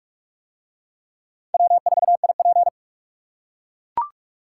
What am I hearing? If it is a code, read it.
Morse code, heard as a keyed tone, about 44 wpm, 700 Hz. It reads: W4IP